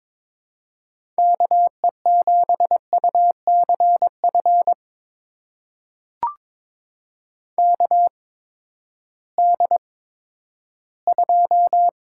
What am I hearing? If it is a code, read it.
Morse code, heard as a keyed tone, about 22 wpm, spaced as for 12 wpm, 700 Hz. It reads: KE7UCF KD2
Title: USA Call Signs - 6 Characters - 8x Character Spacing - 22wpm